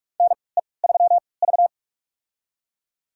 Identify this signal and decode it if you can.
Morse code, heard as a keyed tone, about 46 wpm, spaced as for 26 wpm, 700 Hz. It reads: NE3V
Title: USA Call Signs - 3x Character Spacing - 45wpm